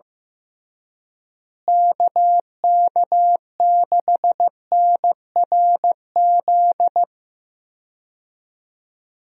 Morse code, heard as a keyed tone, about 15 wpm, 700 Hz. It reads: KK6NRZ
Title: USA Call Signs - 6 Characters - Repeated 2x - 15wpm